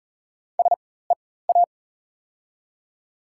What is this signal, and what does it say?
Morse code, heard as a keyed tone, about 38 wpm, spaced as for 18 wpm, 700 Hz. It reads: SEA
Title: 4x Character Spacing - Top 300 Words - 40wpm